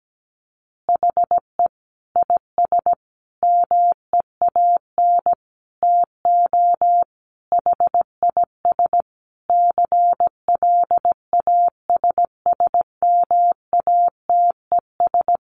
Morse code, heard as a keyed tone, about 17 wpm, 700 Hz. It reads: HE IS MEAN TO HIS CLASSMATES
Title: Additional Easy Phrases 17wpm